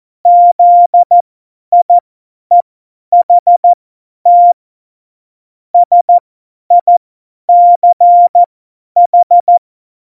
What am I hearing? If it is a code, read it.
Morse code, heard as a keyed tone, about 14 wpm, spaced as for 10 wpm, 700 Hz. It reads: ZIEHT SICH
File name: Morsekurs / Witze / Witz 15. Mai 2018, 50 F70